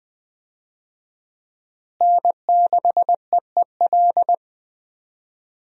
Morse code, heard as a keyed tone, about 21 wpm, 700 Hz. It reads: N6EEL
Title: USA Call Signs - 5 Characters - Repeated 2x - 20wpm